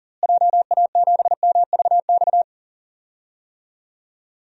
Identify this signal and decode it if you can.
Morse code, heard as a keyed tone, about 40 wpm, 700 Hz. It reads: JA7MVX
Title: DXCC-100 Call Signs - 6 Characters - Repeated 3x - 40wpm